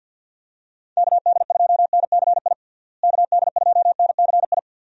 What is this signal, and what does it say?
Morse code, heard as a keyed tone, about 51 wpm, 700 Hz. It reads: KD2NCI KD2NCI
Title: USA Call Signs - 6 Characters - Repeated 2x - 50wpm